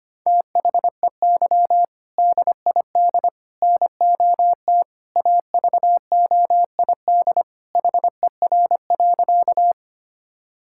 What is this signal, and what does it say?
Morse code, heard as a keyed tone, about 25 wpm, 700 Hz. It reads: THEY DID NOT AVOID HER.
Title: Sentences from Top 1000 Words - Review Part 6 - 25wpm